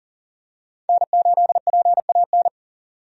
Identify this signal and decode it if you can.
Morse code, heard as a keyed tone, about 37 wpm, 700 Hz. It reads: N8PAN
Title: USA Call Signs - 5 Characters - 40wpm